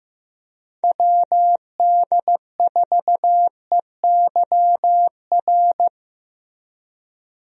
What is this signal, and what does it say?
Morse code, heard as a keyed tone, about 16 wpm, 700 Hz. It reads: WD4EYR
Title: USA Call Signs - 6 Characters - Repeated 3x - 15wpm